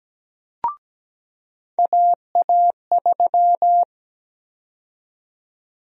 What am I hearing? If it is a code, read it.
Morse code, heard as a keyed tone, about 17 wpm, 700 Hz. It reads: AA3
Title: USA Call Sign Prefixes - 17wpm